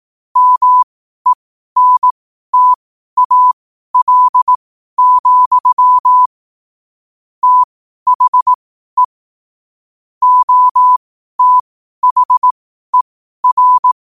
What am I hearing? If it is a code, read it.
Morse code, heard as a keyed tone, about 18 wpm, spaced as for 12 wpm, 1000 Hz. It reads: MENTAL, THE OTHER
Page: Quotes for Thu, 14 Aug 2025 in Morse Code at 12 words per minute.